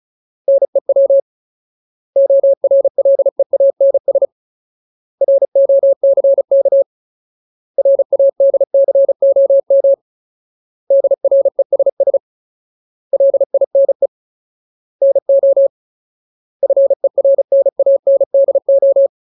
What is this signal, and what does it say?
Morse code, heard as a keyed tone, about 34 wpm, 550 Hz. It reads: NEW ORLEANS ROCK RADCOM DRESS LINE NO FERNANDO
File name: Ultimate Word List / 35WPM Ultimate